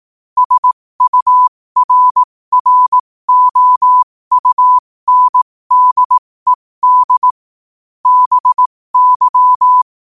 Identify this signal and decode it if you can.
Morse code, heard as a keyed tone, about 18 wpm, spaced as for 15 wpm, 1000 Hz. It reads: SURROUNDED BY
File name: qotdfiles / 15wpm